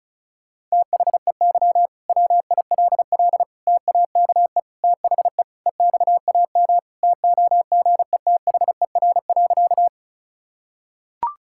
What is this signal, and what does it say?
Morse code, heard as a keyed tone, about 36 wpm, 700 Hz. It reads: THEY WILL TAKE THE EXAM TOGETHER.
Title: Sentences from Top 1000 Words - Review Part 5 - 35wpm